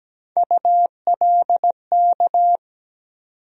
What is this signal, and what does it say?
Morse code, heard as a keyed tone, about 17 wpm, 700 Hz. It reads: ULK
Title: USA Call Sign Postfixes - 3 Letters 17wpm